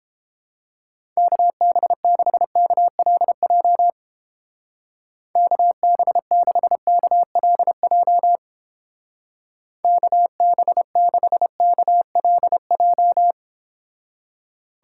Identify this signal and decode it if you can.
Morse code, heard as a keyed tone, about 29 wpm, 700 Hz. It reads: KB6KLJ KB6KLJ KB6KLJ